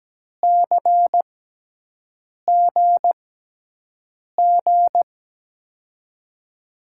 Morse code, heard as a keyed tone, about 17 wpm, spaced as for 6 wpm, 700 Hz. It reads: CGG
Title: USA Call Signs - 6 Characters - 6x Character Spacing - 17wpm